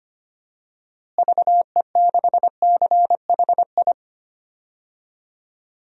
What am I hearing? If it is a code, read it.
Morse code, heard as a keyed tone, about 25 wpm, 700 Hz. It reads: VE6CHI
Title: DXCC-100 Call Signs - 6 Characters - Repeated 2x - 25wpm